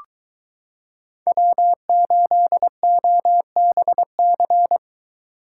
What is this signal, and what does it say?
Morse code, heard as a keyed tone, about 22 wpm, 700 Hz. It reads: W8OBC